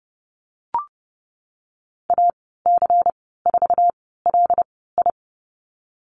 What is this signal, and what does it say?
Morse code, heard as a keyed tone, about 30 wpm, spaced as for 17 wpm, 700 Hz. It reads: AC4LI
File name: Lesson 19g - Letter w - Call Signs 30wpm